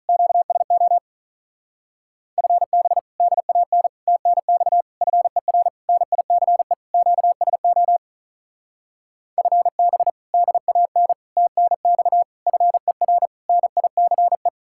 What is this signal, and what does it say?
Morse code, heard as a keyed tone, about 43 wpm, 700 Hz. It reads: QSO FB DAN TNX FER NICE QSO FB DAN TNX FER NICE
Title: Speed-Racing Common QSO Phrases Encore - 35wpm